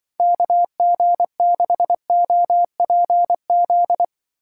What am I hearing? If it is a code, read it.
Morse code, heard as a keyed tone, about 25 wpm, 700 Hz. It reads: KG6OPZ
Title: Speed Racing - USA 2x3 Call Signs 20wpm